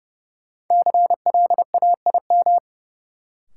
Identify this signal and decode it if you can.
Morse code, heard as a keyed tone, about 30 wpm, 700 Hz. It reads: CLAIM
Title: Sentences from Top 1500 Words - New Word Drill Alt - 30wpm